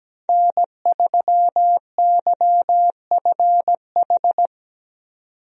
Morse code, heard as a keyed tone, about 17 wpm, 700 Hz. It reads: N3YFH
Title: USA Call Signs - 5 Characters - 17wpm